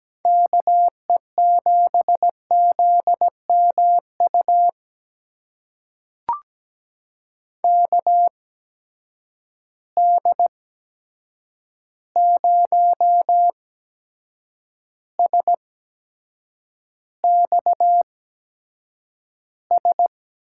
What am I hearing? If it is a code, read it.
Morse code, heard as a keyed tone, about 17 wpm, spaced as for 7 wpm, 700 Hz. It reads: KE7ZMU KD0SXS
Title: USA Call Signs - 6 Characters - 8x Character Spacing - 17wpm